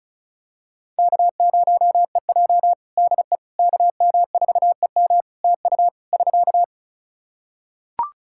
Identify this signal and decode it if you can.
Morse code, heard as a keyed tone, about 34 wpm, 700 Hz. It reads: K0EJ DE KM4EM TU <SK>
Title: Practice QSO - Exchange 4 - Ending - 35wpm